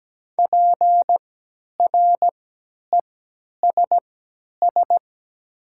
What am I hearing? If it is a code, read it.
Morse code, heard as a keyed tone, about 17 wpm, spaced as for 10 wpm, 700 Hz. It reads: PRESS
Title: 3x Character Spacing - Top 300 Words - 17wpm